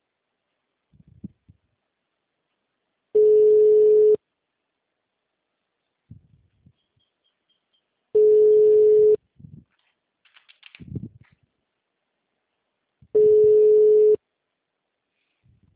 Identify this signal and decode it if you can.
Ringback tone (Europe, 425 Hz).